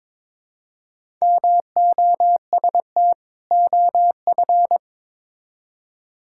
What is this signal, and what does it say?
Morse code, heard as a keyed tone, about 22 wpm, 700 Hz. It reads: MOST OF
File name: Build A Phrase - 2-Word Phrases - 22wpm